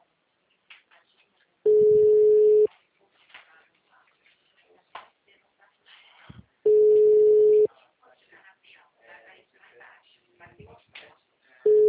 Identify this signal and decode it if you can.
Ringback tone (Europe, 425 Hz), with speech.
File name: out- -2008- - - .19796